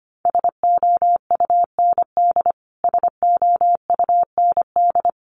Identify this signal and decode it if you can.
Morse code, heard as a keyed tone, about 25 wpm, 700 Hz. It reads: SOUND SOUND
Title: Top 200 Words Repeated 25wpm